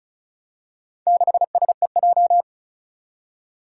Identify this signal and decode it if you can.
Morse code, heard as a keyed tone, about 34 wpm, 700 Hz. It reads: 6SEJ
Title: Four Challenging Letter-Number Combinations 35wpm